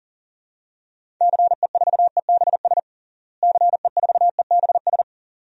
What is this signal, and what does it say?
Morse code, heard as a keyed tone, about 40 wpm, 700 Hz. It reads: CE4EBS CE4EBS
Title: DXCC-100 Call Signs - 6 Characters - Repeated 2x - 40wpm